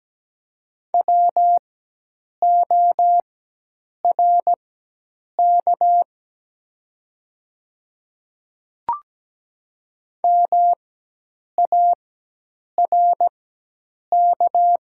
Morse code, heard as a keyed tone, about 17 wpm, spaced as for 8 wpm, 700 Hz. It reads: WORK MARK
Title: 4x Character Spacing - Top 300 Words - 17wpm